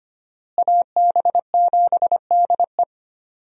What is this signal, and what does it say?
Morse code, heard as a keyed tone, about 25 wpm, 700 Hz. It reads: AB7DE
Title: USA Call Signs - 5 Characters - Repeated 3x - 25wpm